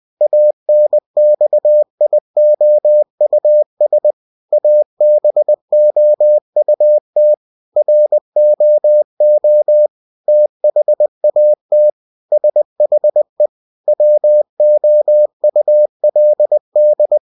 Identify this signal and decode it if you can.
Morse code, heard as a keyed tone, about 20 wpm, 600 Hz. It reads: ANXIOUS ABOUT ROO THAT SHE WOULD
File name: cw-books / winnie-the-pooh / winnie-20wpm-page-0092